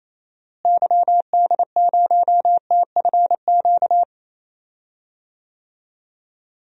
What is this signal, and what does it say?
Morse code, heard as a keyed tone, about 28 wpm, 700 Hz. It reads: YD0TFQ
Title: DXCC-100 Call Signs - 6 Characters - Repeated 3x - 28wpm